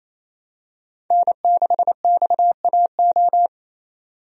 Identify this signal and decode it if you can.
Morse code, heard as a keyed tone, about 28 wpm, 700 Hz. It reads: N6XAO
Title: USA Call Signs - 5 Characters - 28wpm